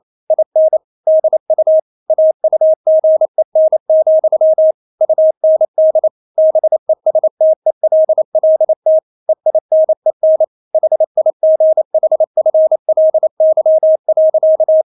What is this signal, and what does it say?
Morse code, heard as a keyed tone, about 28 wpm, 630 Hz. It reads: IN DU AUGEN, UND BESTELLT EINEN HIGHFLY.
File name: Witz9 28wpm